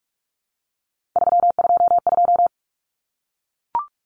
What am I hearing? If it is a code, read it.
Morse code, heard as a keyed tone, about 44 wpm, 700 Hz. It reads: 322
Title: Numbers - 3-Digit 45wpm